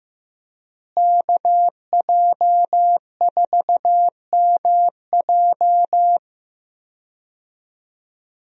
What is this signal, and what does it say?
Morse code, heard as a keyed tone, about 15 wpm, 700 Hz. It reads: KJ4MJ
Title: USA Call Signs - 5 Characters - Repeated 2x - 15wpm